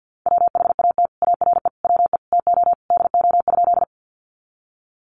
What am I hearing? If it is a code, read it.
Morse code, heard as a keyed tone, about 50 wpm, 700 Hz. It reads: WHAT ARE WE TO DO?
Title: Sentences from Top 100 Words 50wpm